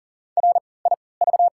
Morse code, heard as a keyed tone, about 40 wpm, spaced as for 23 wpm, 700 Hz. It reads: RIV
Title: 3x Character Spacing - Top 300 Words - 40wpm